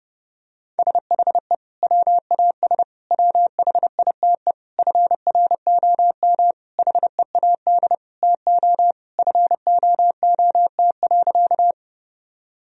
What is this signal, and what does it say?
Morse code, heard as a keyed tone, about 30 wpm, 700 Hz. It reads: SHE WAS WHITE FROM HEAD TO FOOT.
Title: Sentences from Top 1000 Words - Review Part 3 - 30wpm